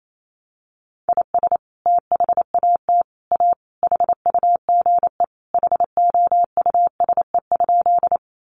Morse code, heard as a keyed tone, about 28 wpm, 700 Hz. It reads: IS THAT A HUGE HOUSE?
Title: Sentences from Top 1100 Words 28wpm